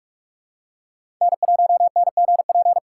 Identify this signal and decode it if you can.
Morse code, heard as a keyed tone, about 46 wpm, 700 Hz. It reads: N1NGP
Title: USA Call Signs - 5 Characters - 4x Character Spacing - 45wpm